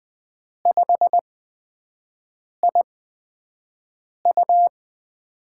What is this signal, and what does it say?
Morse code, heard as a keyed tone, about 20 wpm, spaced as for 5 wpm, 700 Hz. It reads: 5IU
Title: USA Call Signs - 8x Character Spacing - 20wpm